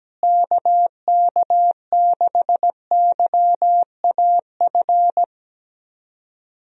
Morse code, heard as a keyed tone, about 17 wpm, 700 Hz. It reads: KK6YAF
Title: USA Call Signs - 6 Characters - Repeated 3x - 17wpm